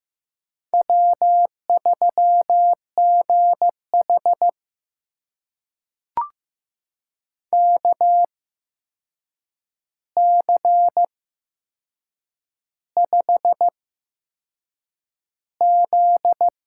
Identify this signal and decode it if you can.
Morse code, heard as a keyed tone, about 15 wpm, spaced as for 6 wpm, 700 Hz. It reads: W3GH KC5Z
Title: USA Call Signs - 8x Character Spacing - 15wpm